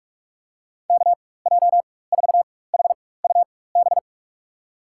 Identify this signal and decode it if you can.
Morse code, heard as a keyed tone, about 45 wpm, spaced as for 21 wpm, 700 Hz. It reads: KJ4HUB
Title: USA Call Signs - 6 Characters - 4x Character Spacing - 45wpm